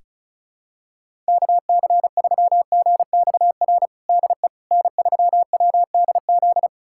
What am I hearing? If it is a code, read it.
Morse code, heard as a keyed tone, about 36 wpm, 700 Hz. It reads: KC3GXR DE N3WDZ
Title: USA Callsign DE Callsign 35wpm